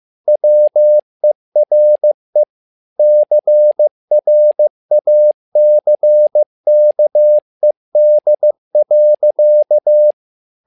Morse code, heard as a keyed tone, about 15 wpm, 600 Hz. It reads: WERE CRACKED.
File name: learning center / tower trouble / tower trouble 15wpm